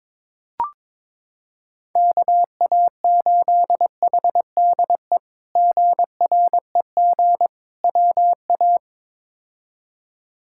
Morse code, heard as a keyed tone, about 22 wpm, 700 Hz. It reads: KA8HDE GREG WA
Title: K1USN SST QSO Practice - 22wpm